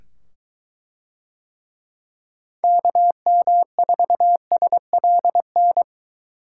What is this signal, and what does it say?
Morse code, heard as a keyed tone, about 23 wpm, 700 Hz. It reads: KM4SLN